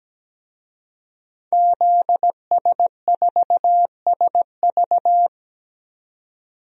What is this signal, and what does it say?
Morse code, heard as a keyed tone, about 17 wpm, 700 Hz. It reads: ZS4SV
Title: DXCC-100 Call Signs - 5 Characters - 17wpm